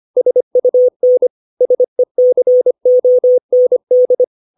Morse code, heard as a keyed tone, about 25 wpm, 500 Hz. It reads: SUN SECOND
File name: 632 words / 632 Words 25WPM